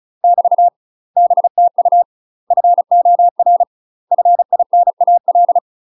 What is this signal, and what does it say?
Morse code, heard as a keyed tone, about 35 wpm, 700 Hz. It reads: <BT> BTU FOR FINAL
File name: UK MASTER QSOs TRAINING/35WPM